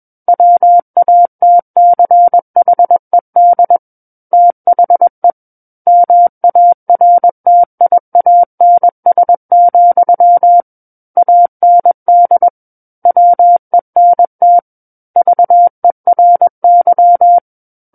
Morse code, heard as a keyed tone, about 21 wpm, 700 Hz. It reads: WATCHED THE MARTIANS, AND WENT VERY
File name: Book One / 12-Chapter 12 - 21 WPM